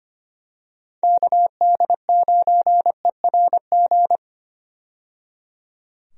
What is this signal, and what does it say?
Morse code, heard as a keyed tone, about 25 wpm, 700 Hz. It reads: KD9ERG
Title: USA Call Signs - 6 Characters - 25wpm